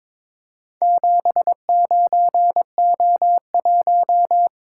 Morse code, heard as a keyed tone, about 22 wpm, 700 Hz. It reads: 79O1
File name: practice / 22wpm / Four Letter-Number Combinations - 22wpm